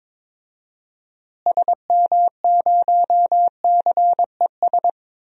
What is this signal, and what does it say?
Morse code, heard as a keyed tone, about 21 wpm, 700 Hz. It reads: SM0CES
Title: DXCC-100 Call Signs - 6 Characters - Repeated 2x - 22wpm